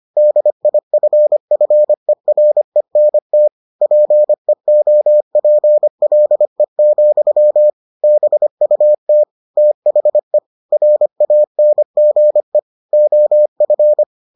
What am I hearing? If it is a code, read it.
Morse code, heard as a keyed tone, about 25 wpm, 600 Hz. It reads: DIFFERENT PEOPLE, BUT THE RANGE OF